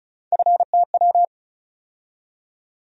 Morse code, heard as a keyed tone, about 36 wpm, 700 Hz. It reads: FTW
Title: USA Call Sign Postfixes - 3 Letters 35wpm